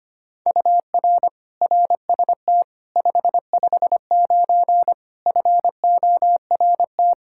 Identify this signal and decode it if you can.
Morse code, heard as a keyed tone, about 25 wpm, 700 Hz. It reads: UR RST 559 FORT
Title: Practice QSO - Exchange 1 - Beginning - 25wpm